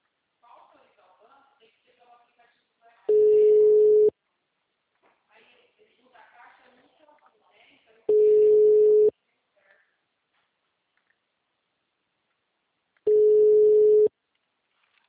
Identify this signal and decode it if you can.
Ringback tone (Europe, 425 Hz), with speech.
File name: out- -2010- - - .11753